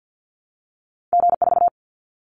Morse code, heard as a keyed tone, about 50 wpm, 700 Hz. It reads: G4
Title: Callsign Prefixes for Top 25 DXCC - Repeated 3x - 50wpm